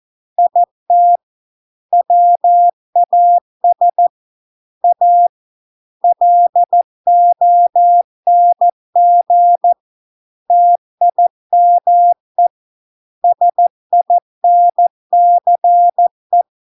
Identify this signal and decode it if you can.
Morse code, heard as a keyed tone, about 14 wpm, 700 Hz. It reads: IT WAS A LONG TIME SINCE